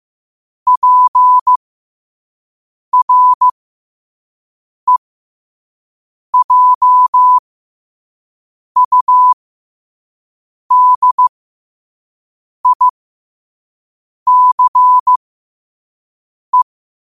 Morse code, heard as a keyed tone, about 15 wpm, spaced as for 5 wpm, 1000 Hz. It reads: PREJUDICE
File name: qotdfiles / 5wpm